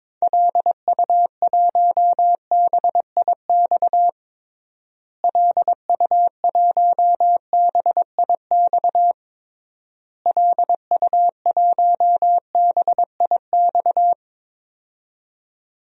Morse code, heard as a keyed tone, about 22 wpm, 700 Hz. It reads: LU1BIX LU1BIX LU1BIX
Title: DXCC-100 Call Signs - 6 Characters - Repeated 3x - 22wpm